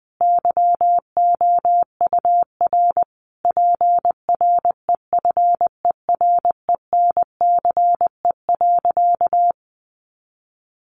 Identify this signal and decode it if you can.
Morse code, heard as a keyed tone, about 20 wpm, 700 Hz. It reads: YOUR PREFERENCE.
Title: Sentences from Top 1500 Words 20wpm